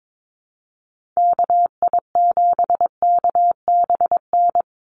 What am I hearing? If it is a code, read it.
Morse code, heard as a keyed tone, about 22 wpm, 700 Hz. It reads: KI7KBN
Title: USA Call Signs 22wpm